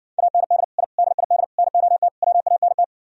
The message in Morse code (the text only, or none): AND I DID NOT WANT